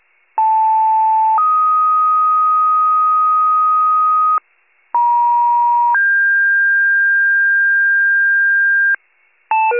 Two-tone page, heard as a one-second tone, then a three-second tone, about 880 then 1250 Hz, about 950 then 1600 Hz, with noise. It is heard from 0.4 to 9.0 s.